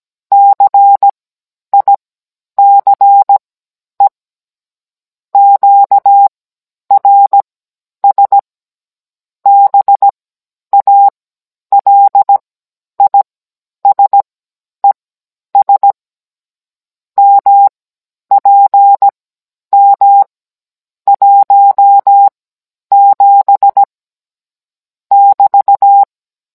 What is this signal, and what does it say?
Morse code, heard as a keyed tone, about 17 wpm, spaced as for 10 wpm, 800 Hz. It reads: CICE QRS BALISES MPM 17 <BT>
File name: Cours-NR19-16mm